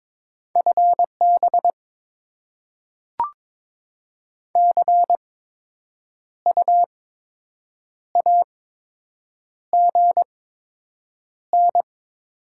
Morse code, heard as a keyed tone, about 22 wpm, spaced as for 7 wpm, 700 Hz. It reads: FB CU AGN
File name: Top 20 QSO Elements - 8x Character Spacing - 22wpm